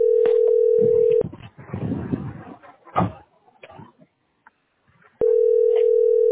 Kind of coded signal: telephone call progress tone